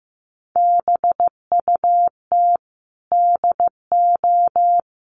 Morse code, heard as a keyed tone, about 15 wpm, 700 Hz. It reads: BUT DO
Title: Sentences from Top 1500 Words 15wpm